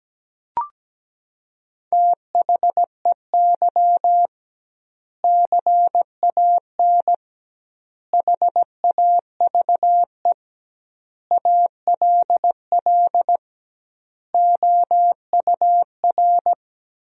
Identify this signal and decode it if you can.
Morse code, heard as a keyed tone, about 17 wpm, 700 Hz. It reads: THEY CAN HAVE ALL OUR
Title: 2x Word-Spacing - Sentences from Top 400 Words - 17wpm